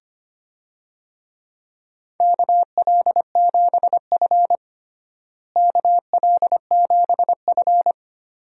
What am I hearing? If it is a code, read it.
Morse code, heard as a keyed tone, about 25 wpm, 700 Hz. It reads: KL7F KL7F
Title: DXCC-100 Call Signs - 4 Characters - Repeated 3x - 25wpm